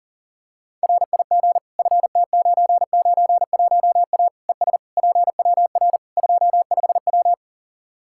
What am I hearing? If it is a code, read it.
Morse code, heard as a keyed tone, about 40 wpm, 700 Hz. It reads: RIG FT991A ES PWR 25W
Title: Practice QSO - Exchange 2 - Info - 40wpm